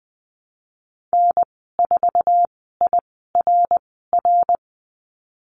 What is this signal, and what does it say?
Morse code, heard as a keyed tone, about 19 wpm, spaced as for 14 wpm, 700 Hz. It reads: N4IRR
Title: 088 Lesson 15g - Number 5 - Call Signs 20wpm